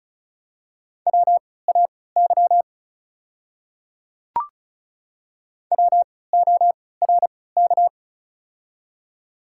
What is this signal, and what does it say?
Morse code, heard as a keyed tone, about 36 wpm, spaced as for 20 wpm, 700 Hz. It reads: WAY WORK
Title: 3x Character Spacing - Top 200 Words - 35wpm